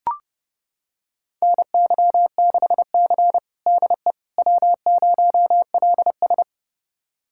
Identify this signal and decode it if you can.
Morse code, heard as a keyed tone, about 31 wpm, 700 Hz. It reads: NY6C DE W0LS
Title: USA Callsign DE Callsign - 4 Characters - 30wpm